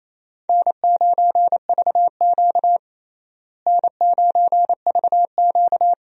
Morse code, heard as a keyed tone, about 28 wpm, 700 Hz. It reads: N9VQ N9VQ
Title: USA Call Signs - 4 Characters - Repeated 2x - 28wpm